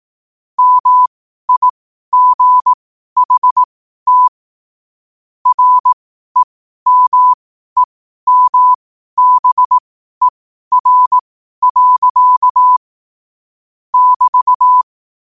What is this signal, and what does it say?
Morse code, heard as a keyed tone, about 18 wpm, spaced as for 12 wpm, 1000 Hz. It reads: MIGHT REMEMBER. <BT>
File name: qotdfiles / 12wpm